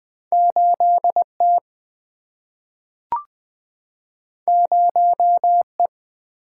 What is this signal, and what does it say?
Morse code, heard as a keyed tone, about 20 wpm, 700 Hz. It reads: 8T 0E
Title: Single-Letter Single-Number 20wpm